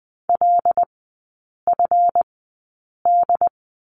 Morse code, heard as a keyed tone, about 20 wpm, spaced as for 8 wpm, 700 Hz. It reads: LFD
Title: 138 Lesson 23e - Letter f - Three Characters 20wpm